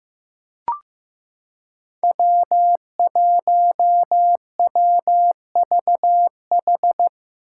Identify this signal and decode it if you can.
Morse code, heard as a keyed tone, about 15 wpm, 700 Hz. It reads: W1WVH
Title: Five Challenging Letter-Number Combinations 15wpm